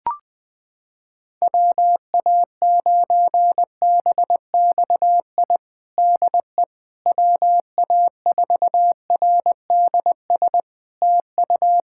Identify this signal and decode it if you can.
Morse code, heard as a keyed tone, about 20 wpm, 700 Hz. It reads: WA9BXI DE WA4RDS TU
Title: Practice QSO - Exchange 4 - Ending - 20wpm